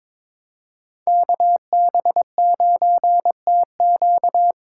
Morse code, heard as a keyed tone, about 26 wpm, 700 Hz. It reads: KB9TQ
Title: Speed Racing - USA 2x2 Call Signs 22wpm